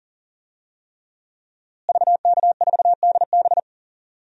Morse code, heard as a keyed tone, about 37 wpm, 700 Hz. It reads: VK4DB
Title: DXCC-100 Call Signs - 5 Characters - Repeated 3x - 40wpm